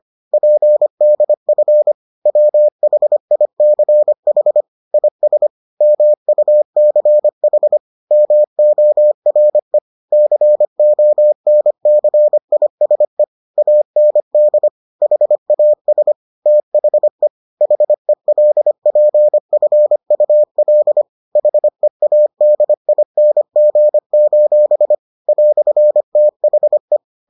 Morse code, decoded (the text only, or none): PDF WHICH IS MUCH MORE CONCISE AND HAS THE HELPFUL HEADING: "THE